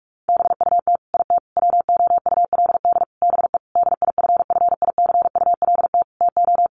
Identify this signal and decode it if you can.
Morse code, heard as a keyed tone, about 44 wpm, 700 Hz. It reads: BUT IT WOULD BE DIFFICULT TO
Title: Sentences from Top 600 Words 45wpm